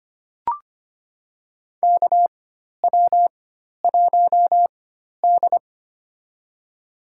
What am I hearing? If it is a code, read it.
Morse code, heard as a keyed tone, about 25 wpm, spaced as for 12 wpm, 700 Hz. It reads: KW1D
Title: USA Call Signs - 4x Character Spacing - 25wpm